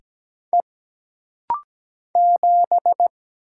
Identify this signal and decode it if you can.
Morse code, heard as a keyed tone, about 17 wpm, 700 Hz. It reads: E 7
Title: Single Letter-Number - Rapid-Fire 17wpm